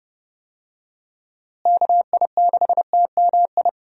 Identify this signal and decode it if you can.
Morse code, heard as a keyed tone, about 32 wpm, 700 Hz. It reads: KI6TMI
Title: USA Call Signs - 6 Characters - Repeated 3x - 30wpm